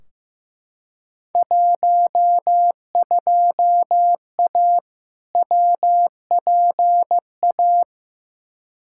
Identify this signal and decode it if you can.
Morse code, heard as a keyed tone, about 15 wpm, 700 Hz. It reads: 12A WPA
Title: Field Day Exchanges Encore - 15wpm